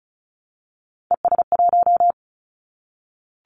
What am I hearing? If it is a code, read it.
Morse code, heard as a keyed tone, about 34 wpm, 700 Hz. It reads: ES1
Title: Callsign Prefixes for Top 100 DXCC - 35wpm